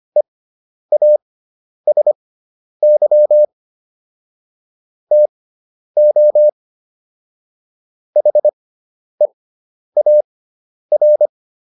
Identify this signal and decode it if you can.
Morse code, heard as a keyed tone, about 25 wpm, spaced as for 10 wpm, 600 Hz. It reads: EASY TO HEAR